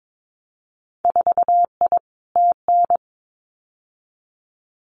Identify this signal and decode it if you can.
Morse code, heard as a keyed tone, about 22 wpm, 700 Hz. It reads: 4I TN
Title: Winter Field Day Exchanges 22wpm